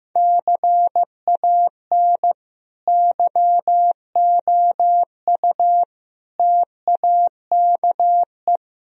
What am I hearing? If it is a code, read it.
Morse code, heard as a keyed tone, about 15 wpm, 700 Hz. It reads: CAN YOU TAKE
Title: Sentences from Top 1000 Words - Review Part 3 - 15wpm